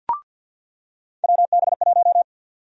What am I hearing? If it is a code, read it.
Morse code, heard as a keyed tone, about 46 wpm, 700 Hz. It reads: WB1
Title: USA Call Sign Prefixes - 50wpm